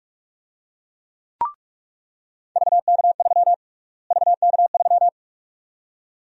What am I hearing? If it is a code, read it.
Morse code, heard as a keyed tone, about 46 wpm, 700 Hz. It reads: VK3 VK3
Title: USA Call Sign Prefixes - Repeated 2x - 45wpm